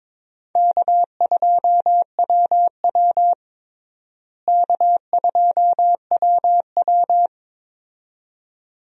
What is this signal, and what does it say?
Morse code, heard as a keyed tone, about 22 wpm, 700 Hz. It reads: K2WW K2WW
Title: USA Call Signs - 4 Characters - Repeated 2x - 22wpm